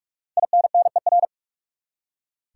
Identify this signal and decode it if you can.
Morse code, heard as a keyed tone, about 47 wpm, 700 Hz. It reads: INNER
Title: Five Letter Words - 45wpm